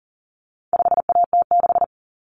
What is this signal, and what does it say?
Morse code, heard as a keyed tone, about 39 wpm, 700 Hz. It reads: 5AT6
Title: Common Cut Numbers - 4-Digit 40wpm